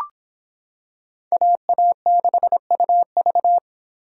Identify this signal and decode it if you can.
Morse code, heard as a keyed tone, about 25 wpm, 700 Hz. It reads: AA6UV